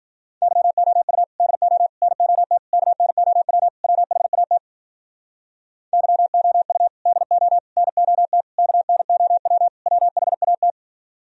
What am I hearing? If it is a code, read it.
Morse code, heard as a keyed tone, about 46 wpm, 700 Hz. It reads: YOU DO NOT KNOW WHAT YOU DO NOT KNOW WHAT
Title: Speed-Racing - Sets of 5 Words 40wpm